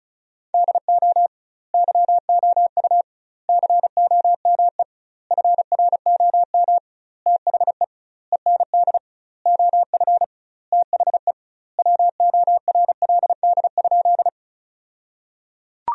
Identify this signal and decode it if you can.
Morse code, heard as a keyed tone, about 36 wpm, 700 Hz. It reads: DO YOU COME FROM THE END OF THE WORLD?
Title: 2x Word-Spacing - Sentences from Top 200 Words - 35wpm